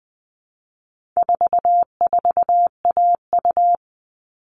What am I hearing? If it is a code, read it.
Morse code, heard as a keyed tone, about 20 wpm, 700 Hz. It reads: 44AU
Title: All Cut Numbers - 4-Digit 20wpm